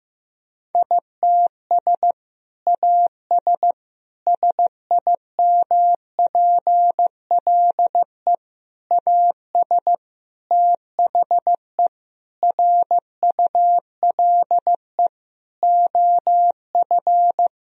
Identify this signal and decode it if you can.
Morse code, heard as a keyed tone, about 15 wpm, 700 Hz. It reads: ITS AS SIMPLE AS THE RULE OF
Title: Sentences from Top 1000 Words - Review Part 2 - 15wpm